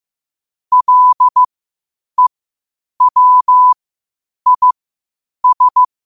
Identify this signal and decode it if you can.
Morse code, heard as a keyed tone, about 15 wpm, spaced as for 8 wpm, 1000 Hz. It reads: LEWIS
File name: qotdfiles / 8wpm